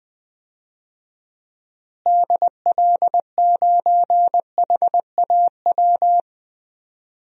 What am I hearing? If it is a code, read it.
Morse code, heard as a keyed tone, about 20 wpm, 700 Hz. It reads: DL9HAW
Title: DXCC-100 Call Signs - 6 Characters - Repeated 3x - 20wpm